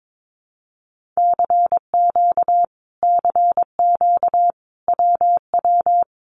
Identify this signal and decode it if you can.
Morse code, heard as a keyed tone, about 22 wpm, 700 Hz. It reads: CQ CQ WW
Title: Common QSO Phrases Encore 22wpm